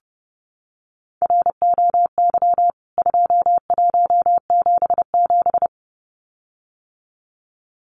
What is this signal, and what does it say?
Morse code, heard as a keyed tone, about 30 wpm, 700 Hz. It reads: ROY 2177
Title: CWT Exchange 30wpm